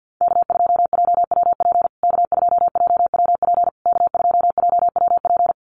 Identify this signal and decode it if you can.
Morse code, heard as a keyed tone, about 49 wpm, 700 Hz. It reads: K2JWP K2JWP K2JWP
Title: USA Call Signs Repeated 3x 50wpm